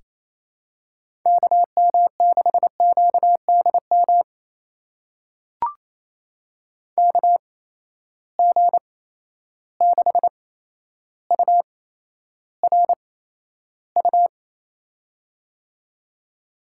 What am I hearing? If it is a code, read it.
Morse code, heard as a keyed tone, about 28 wpm, spaced as for 12 wpm, 700 Hz. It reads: KM6QDM KG6URU
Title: USA Call Signs - 6 Characters - 8x Character Spacing - 28wpm